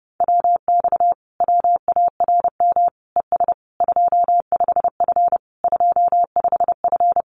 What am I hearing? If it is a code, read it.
Morse code, heard as a keyed tone, about 30 wpm, 700 Hz. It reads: WX WARM ES 25F 25F
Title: Common QSO Phrases Encore 30wpm